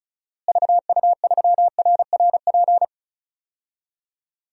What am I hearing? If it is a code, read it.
Morse code, heard as a keyed tone, about 34 wpm, 700 Hz. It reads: VU3RRP
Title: DXCC-100 Call Signs - 6 Characters - Repeated 2x - 35wpm